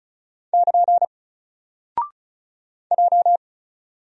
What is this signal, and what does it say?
Morse code, heard as a keyed tone, about 36 wpm, 700 Hz. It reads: <KN> J
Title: Single Character (Letter:Number:Punctuation:Prosign) 35wpm